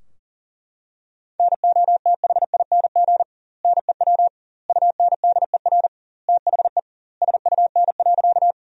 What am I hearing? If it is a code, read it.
Morse code, heard as a keyed tone, about 37 wpm, 700 Hz. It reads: NOTHING NEW UNDER THE SUN.
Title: 2x Word-Spacing - Sentences from Top 500 Words - 40wpm